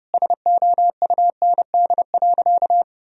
Morse code, heard as a keyed tone, about 30 wpm, 700 Hz. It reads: SOUND.
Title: Sentences from Top 1000 Words - Review Part 1 - 30wpm